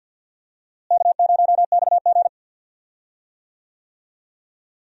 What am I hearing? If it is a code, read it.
Morse code, heard as a keyed tone, about 50 wpm, 700 Hz. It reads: K0XG
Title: USA Call Signs - 4 Characters - Repeated 3x - 50wpm